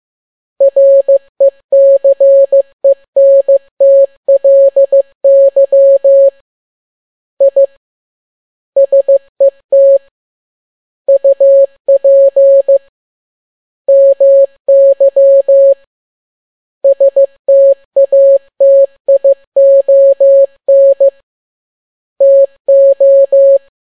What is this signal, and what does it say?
Morse code, heard as a keyed tone, about 15 wpm, 550 Hz. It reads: RECENTLY I SET UP MY STATION TO